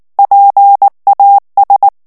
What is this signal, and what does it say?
Morse code, heard as a keyed tone, about 19 wpm, 800 Hz. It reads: PAS
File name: CW / FICHIERS-MP3 / FRANCAIS TEXTE CLAIR 2 VITESSE 18 MOTS MINUTES